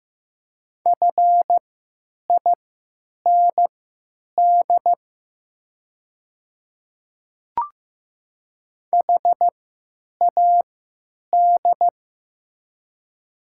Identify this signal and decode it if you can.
Morse code, heard as a keyed tone, about 15 wpm, spaced as for 9 wpm, 700 Hz. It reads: FIND HAD
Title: 3x Character Spacing - Top 300 Words - 15wpm